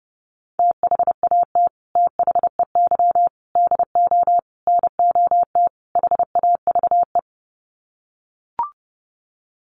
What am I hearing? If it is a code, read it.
Morse code, heard as a keyed tone, about 30 wpm, 700 Hz. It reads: THAT THEY DO NOT HAVE
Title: Sets of 5 Words Encore2 30wpm